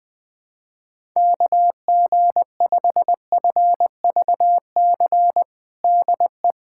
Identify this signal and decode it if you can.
Morse code, heard as a keyed tone, about 20 wpm, 700 Hz. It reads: KG5FVC DE
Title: USA Callsign DE Callsign 20wpm